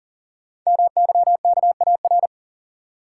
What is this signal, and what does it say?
Morse code, heard as a keyed tone, about 42 wpm, 690 Hz. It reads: MYKAR
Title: Code Groups - 5 Letters - 40wpm